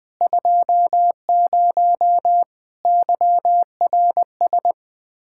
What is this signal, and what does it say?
Morse code, heard as a keyed tone, about 21 wpm, 700 Hz. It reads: 20 YRS
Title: Practice QSO - Exchange 3 - More Info - 20wpm